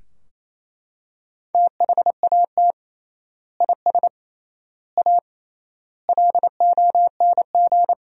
Morse code, heard as a keyed tone, about 28 wpm, 700 Hz. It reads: THAT IS A LONG
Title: 3x Word-Spacing - Sentences from Top 100 Words - 28wpm